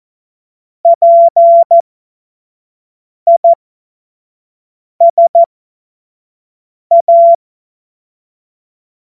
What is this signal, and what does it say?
Morse code, heard as a keyed tone, about 14 wpm, spaced as for 5 wpm, 680 Hz. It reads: PISA
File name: Morsekurs / Witze / Witz 18. Januar 2022, 25 F70